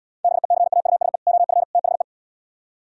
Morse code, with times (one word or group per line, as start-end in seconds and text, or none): 0.2-1.2 BECAUSE
1.3-1.6 OF
1.7-2.0 THE